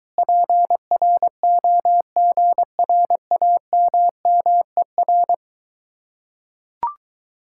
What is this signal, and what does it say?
Morse code, heard as a keyed tone, about 23 wpm, 700 Hz. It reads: PROGRAMMER